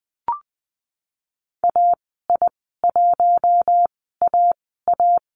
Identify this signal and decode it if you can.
Morse code, heard as a keyed tone, about 20 wpm, spaced as for 15 wpm, 700 Hz. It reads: AI1AA
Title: 042 Lesson 8g - Number 1 - Call Signs 20wpm